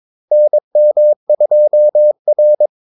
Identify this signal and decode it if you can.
Morse code, heard as a keyed tone, about 22 wpm, 600 Hz. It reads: NM2R
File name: Callsigns-22wpm